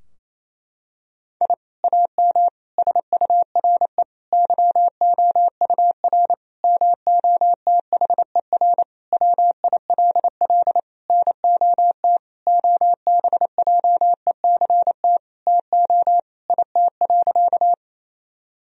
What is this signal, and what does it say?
Morse code, heard as a keyed tone, about 28 wpm, 700 Hz. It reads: I AM SURE YOUR MOTHER WILL NOT OBJECT TO IT.
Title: Sentences from Top 1000 Words - Review Part 3 - 28wpm